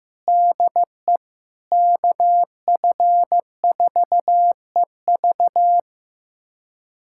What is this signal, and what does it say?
Morse code, heard as a keyed tone, about 15 wpm, 700 Hz. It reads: DE KF4EV
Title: USA Callsign DE Callsign - 5 Characters - 15wpm